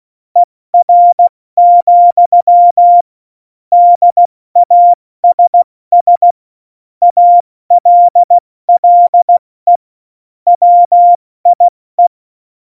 Morse code, heard as a keyed tone, about 16 wpm, spaced as for 14 wpm, 700 Hz. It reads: ER, DASS ALLE WIE
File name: Morsekurs / Witze / Witz 17. November 2020, 70 F80